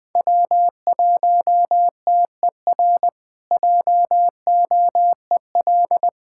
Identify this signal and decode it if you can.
Morse code, heard as a keyed tone, about 20 wpm, 700 Hz. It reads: W1TER JOEL
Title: K1USN SST QSO Practice - 20wpm